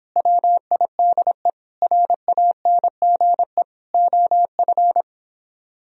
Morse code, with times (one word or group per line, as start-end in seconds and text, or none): 0.2-1.5 WIDE
1.8-3.6 RANGE
3.9-5.0 OF